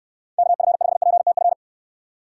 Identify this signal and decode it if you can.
Morse code, heard as a keyed tone, about 64 wpm, 700 Hz. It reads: KF5PTF